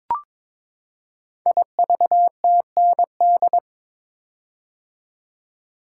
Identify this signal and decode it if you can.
Morse code, heard as a keyed tone, about 22 wpm, 700 Hz. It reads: IVTND
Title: Code Groups - 5 Letters - 22wpm